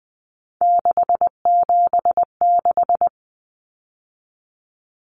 Morse code, heard as a keyed tone, about 20 wpm, 700 Hz. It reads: 676
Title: Numbers - 3-Digit 20wpm